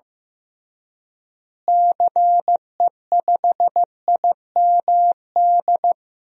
Morse code, heard as a keyed tone, about 15 wpm, 700 Hz. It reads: CE5IMD
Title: DXCC-100 Call Signs - 6 Characters - Repeated 2x - 15wpm